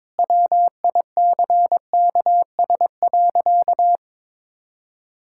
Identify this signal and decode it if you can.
Morse code, heard as a keyed tone, about 22 wpm, 700 Hz. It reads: WICKS.
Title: Pangram Challenge v2 - 22wpm